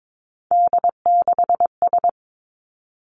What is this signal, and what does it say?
Morse code, heard as a keyed tone, about 22 wpm, 700 Hz. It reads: D6S
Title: Three Challenging Letter-Number Combinations 22wpm